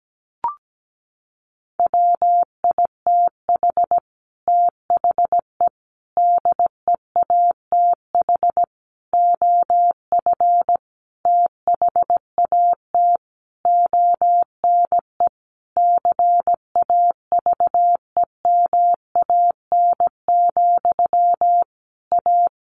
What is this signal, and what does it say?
Morse code, heard as a keyed tone, about 17 wpm, 700 Hz. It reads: WITH THE DEATH OF THAT ONE CAVEMAN, A
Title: A Sound Of Thunder 17wpm